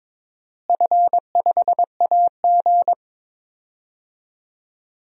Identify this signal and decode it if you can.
Morse code, heard as a keyed tone, about 22 wpm, 700 Hz. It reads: F5AG
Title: DXCC-100 Call Signs - 4 Characters - Repeated 3x - 22wpm